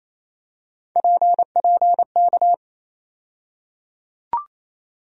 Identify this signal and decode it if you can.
Morse code, heard as a keyed tone, about 28 wpm, 700 Hz. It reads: PPK
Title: USA Call Sign Postfixes - 3 Letters 28wpm